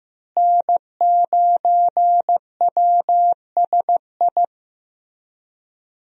Morse code, heard as a keyed tone, about 15 wpm, 700 Hz. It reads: N9WSI
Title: USA Call Signs - 5 Characters - Repeated 3x - 15wpm